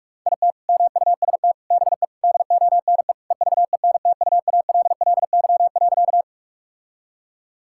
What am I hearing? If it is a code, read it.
Morse code, heard as a keyed tone, about 46 wpm, 700 Hz. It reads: IT MUST BE DONE EVENTUALLY.
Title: Sentences from Top 1000 Words - Review Part 6 - 45wpm